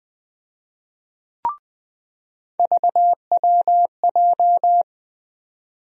Morse code, heard as a keyed tone, about 20 wpm, 700 Hz. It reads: VWJ
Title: USA Call Sign Postfixes - 3 Letters 20wpm